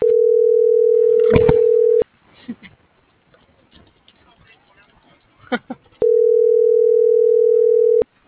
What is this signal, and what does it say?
Ringback tone (North America), 440 + 480 Hz.